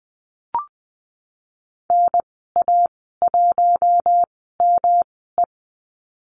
Morse code, heard as a keyed tone, about 20 wpm, spaced as for 15 wpm, 700 Hz. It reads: NA1ME
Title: 122 Lesson 20g - Number 3 - Call Signs 20wpm